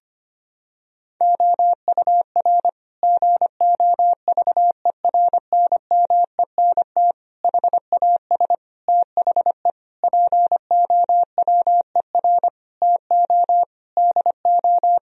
Morse code, heard as a keyed tone, about 25 wpm, 700 Hz. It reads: OUR GOVERNMENT HAS THE POWER TO DO
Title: Sentences from Top 1000 Words - Review Part 3 - 25wpm